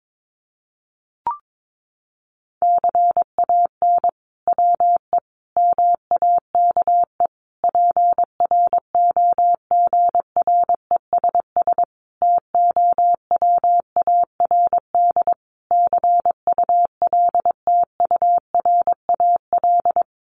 Morse code, heard as a keyed tone, about 22 wpm, 700 Hz. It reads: CAN WE MAKE PROGRESS TOWARD CULTURAL
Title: Sentences from Top 1500 Words 22wpm